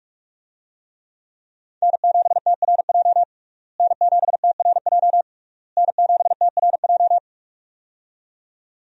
Morse code, heard as a keyed tone, about 46 wpm, 700 Hz. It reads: N7TRJ N7TRJ N7TRJ
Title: USA Call Signs - 5 Characters - Repeated 3x - 45wpm